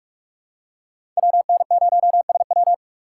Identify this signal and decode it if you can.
Morse code, heard as a keyed tone, about 45 wpm, 700 Hz. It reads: WN0SW
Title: USA Call Signs - 5 Characters - 2x Character Spacing - 45wpm